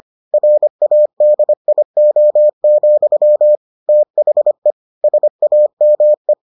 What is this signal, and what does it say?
Morse code, heard as a keyed tone, about 25 wpm, 600 Hz. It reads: RADIO, THE SAME